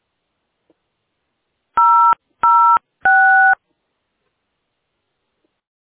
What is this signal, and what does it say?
DTMF, heard as telephone touch tones, tones of about 390 ms. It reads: 006